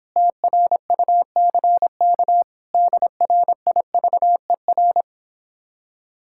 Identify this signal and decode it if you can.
Morse code, heard as a keyed tone, about 26 wpm, 700 Hz. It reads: TRUCK DRIVER